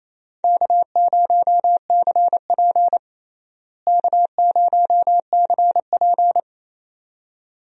Morse code, heard as a keyed tone, about 28 wpm, 700 Hz. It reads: K0CP K0CP
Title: USA Call Signs - 4 Characters - Repeated 2x - 28wpm